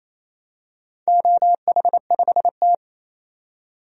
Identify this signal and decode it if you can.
Morse code, heard as a keyed tone, about 28 wpm, 700 Hz. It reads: OH5T
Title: DXCC-100 Call Signs - 4 Characters - 28wpm